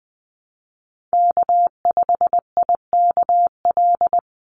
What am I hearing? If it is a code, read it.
Morse code, heard as a keyed tone, about 20 wpm, 700 Hz. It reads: K5IKL
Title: USA Call Signs 20wpm